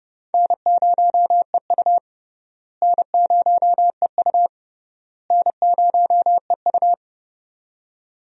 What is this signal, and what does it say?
Morse code, heard as a keyed tone, about 32 wpm, 700 Hz. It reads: N0EU N0EU N0EU
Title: USA Call Signs - 4 Characters - Repeated 3x - 30wpm